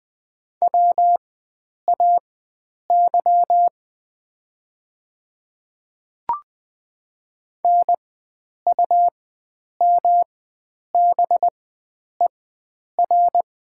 Morse code, heard as a keyed tone, about 20 wpm, spaced as for 9 wpm, 700 Hz. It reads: WAY NUMBER
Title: 4x Character Spacing - Top 100 Words - 20wpm